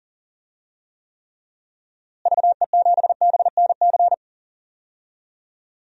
Morse code, heard as a keyed tone, about 40 wpm, 700 Hz. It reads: VE7BNC
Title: DXCC-100 Call Signs - 6 Characters - Repeated 2x - 40wpm